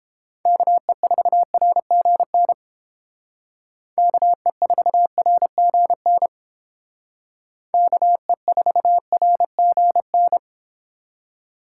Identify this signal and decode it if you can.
Morse code, heard as a keyed tone, about 29 wpm, 700 Hz. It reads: KE4RGN KE4RGN KE4RGN